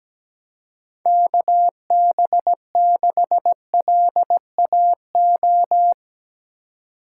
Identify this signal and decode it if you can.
Morse code, heard as a keyed tone, about 17 wpm, 700 Hz. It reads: KB6LAO
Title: USA Call Signs - 6 Characters - 4x Character Spacing - 17wpm